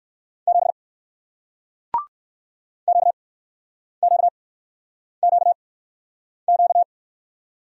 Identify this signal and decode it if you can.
Morse code, heard as a keyed tone, about 62 wpm, spaced as for 9 wpm, 700 Hz. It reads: 7 QQQQ